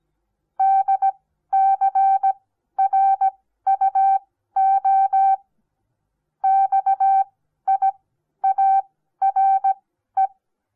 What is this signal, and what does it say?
Morse code, heard as a keyed tone, about 17 wpm, spaced as for 12 wpm, 780 Hz. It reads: DCRUO XIARE
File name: Onlinelehrgang / morsen / DJ4UF Morsen08